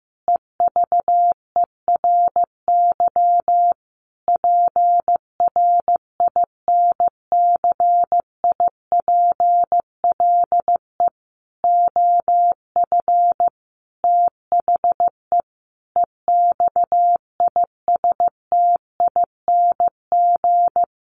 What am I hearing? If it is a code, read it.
Morse code, heard as a keyed tone, about 15 wpm, 700 Hz. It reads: EVERY PRINCIPLE OF THE EXISTING
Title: Sentences from Top 1500 Words 15wpm